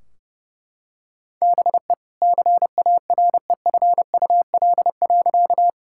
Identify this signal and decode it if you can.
Morse code, heard as a keyed tone, about 30 wpm, 700 Hz. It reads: BE CAREFUL.
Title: Sentences from Top 1600 Words 30wpm